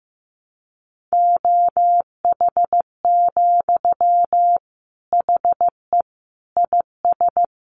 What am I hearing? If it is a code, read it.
Morse code, heard as a keyed tone, about 15 wpm, 700 Hz. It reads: OH, HE IS
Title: Sentences from Top 500 Words 15wpm